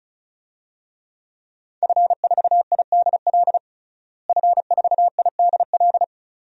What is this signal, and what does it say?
Morse code, heard as a keyed tone, about 34 wpm, 700 Hz. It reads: F4IDL F4IDL
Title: DXCC-100 Call Signs - 5 Characters - Repeated 2x - 35wpm